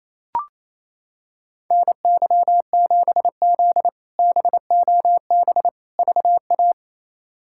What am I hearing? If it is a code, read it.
Morse code, heard as a keyed tone, about 28 wpm, 700 Hz. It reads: NY7Z BOB VA
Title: K1USN SST QSO Practice - 28wpm